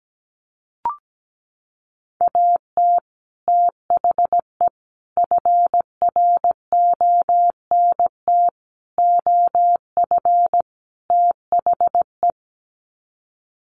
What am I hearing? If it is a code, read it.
Morse code, heard as a keyed tone, about 17 wpm, 700 Hz. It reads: AT THE FRONT OF THE
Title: Sets of 5 Words 17wpm